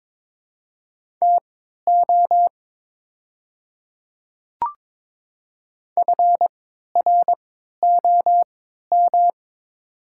Morse code, heard as a keyed tone, about 22 wpm, spaced as for 13 wpm, 690 Hz. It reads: TO FROM
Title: 3x Character Spacing - Top 300 Words - 22wpm